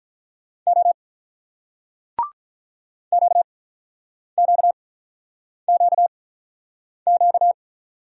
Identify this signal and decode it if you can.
Morse code, heard as a keyed tone, about 47 wpm, spaced as for 9 wpm, 700 Hz. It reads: O QQQQ